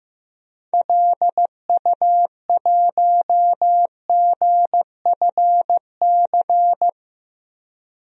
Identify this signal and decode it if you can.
Morse code, heard as a keyed tone, about 15 wpm, 700 Hz. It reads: LU1GFC
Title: DXCC-100 Call Signs - 6 Characters - 15wpm